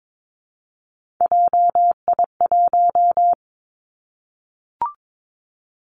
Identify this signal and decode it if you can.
Morse code, heard as a keyed tone, about 22 wpm, 700 Hz. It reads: JI1
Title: Three Challenging Letter-Number Combinations 22wpm